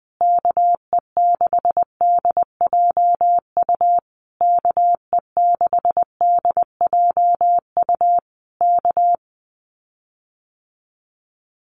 Morse code, heard as a keyed tone, about 20 wpm, 700 Hz. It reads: KE6DJU KE6DJU K
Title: Common QSO Phrases Encore 20wpm